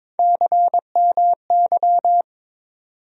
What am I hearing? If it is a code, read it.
Morse code, heard as a keyed tone, about 22 wpm, 700 Hz. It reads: CMY
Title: USA Call Sign Postfixes - 3 Letters 22wpm